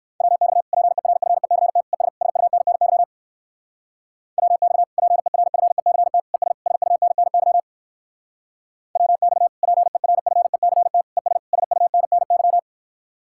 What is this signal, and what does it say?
Morse code, heard as a keyed tone, about 65 wpm, 700 Hz. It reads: WX PERFECT ES SUNNY WX PERFECT ES SUNNY WX PERFECT ES SUNNY